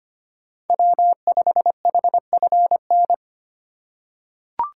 Morse code, heard as a keyed tone, about 25 wpm, 700 Hz. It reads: W5HFN
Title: USA Call Signs - 4x Character Spacing - 25wpm